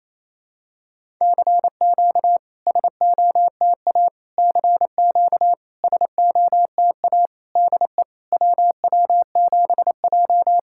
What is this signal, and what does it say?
Morse code, heard as a keyed tone, about 28 wpm, 690 Hz. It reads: CQ SOTA CQ SOTA DE WW7J
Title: SOTA QSO Practice 28wpm